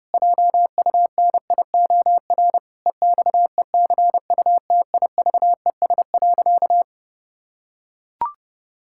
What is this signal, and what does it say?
Morse code, heard as a keyed tone, about 30 wpm, 700 Hz. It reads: JUNIOR EXECUTIVES.
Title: Pangram Challenge v2 - 30wpm